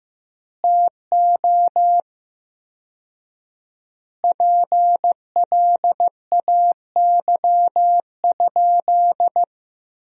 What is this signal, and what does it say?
Morse code, heard as a keyed tone, about 16 wpm, 700 Hz. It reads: TO PLAY?
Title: 4x Word-Spacing - Sentences from Top 400 Words - 15wpm